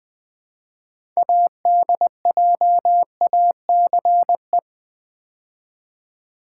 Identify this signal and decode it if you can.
Morse code, heard as a keyed tone, about 20 wpm, 700 Hz. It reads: ADJACE
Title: Build A Word - 2 Letters At A Time - 8-Letter Words - 20wpm